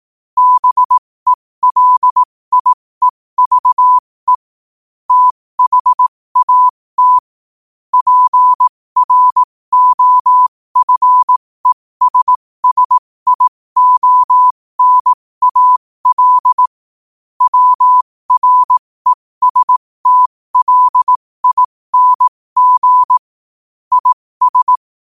Morse code, heard as a keyed tone, about 18 wpm, spaced as for 15 wpm, 1000 Hz. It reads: BELIEVE THAT PROFESSIONAL WRESTLING IS
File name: qotdfiles / 15wpm